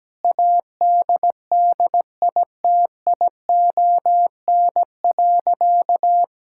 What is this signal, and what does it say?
Morse code, heard as a keyed tone, about 17 wpm, 700 Hz. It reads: ADDITION.
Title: Sentences from Top 1000 Words - Review Part 6 - 17wpm